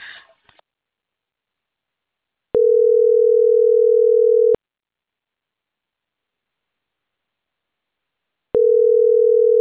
Ringback tone (North America), 440 + 480 Hz.